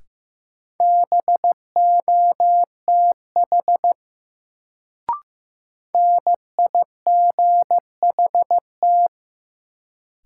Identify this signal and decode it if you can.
Morse code, heard as a keyed tone, about 15 wpm, 700 Hz. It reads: BOTH NIGHT
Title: Sets of 4 Words - New Word Drill - 15wpm